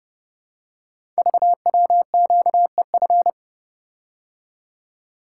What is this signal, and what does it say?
Morse code, heard as a keyed tone, about 30 wpm, 700 Hz. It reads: VWQEF
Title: Code Groups - 5 Letters - 30wpm